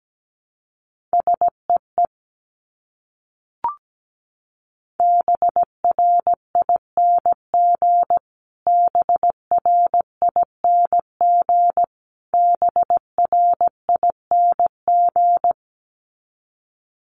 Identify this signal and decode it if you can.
Morse code, heard as a keyed tone, about 17 wpm, 700 Hz. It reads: SEE BRING BRING BRING
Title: Top 500 Words Repeated 17wpm